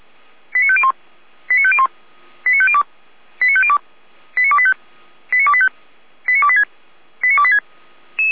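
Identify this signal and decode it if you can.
Five-tone selective call (ZVEI), heard as quick runs of five tones, tones of about 70 ms. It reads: 89501 89501 89502 89502 89285 89285 89286 89286
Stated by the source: ZVEI five-tone selective call received on SDR.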